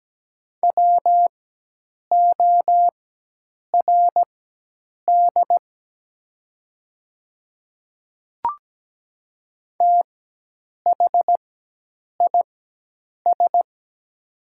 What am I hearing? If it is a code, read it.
Morse code, heard as a keyed tone, about 17 wpm, spaced as for 8 wpm, 700 Hz. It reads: WORD THIS
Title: 4x Character Spacing - Top 100 Words - 17wpm